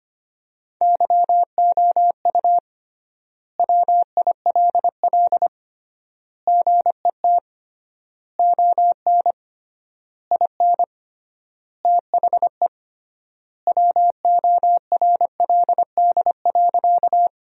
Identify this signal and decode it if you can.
Morse code, heard as a keyed tone, about 25 wpm, 700 Hz. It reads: YOU WILL GET ON IN THE WORLD.
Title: 3x Word-Spacing - Sentences from Top 200 Words - 25wpm